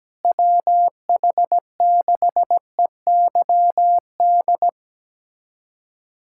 Morse code, heard as a keyed tone, about 17 wpm, 700 Hz. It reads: WH6EYD
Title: USA Callsign DE Callsign 17wpm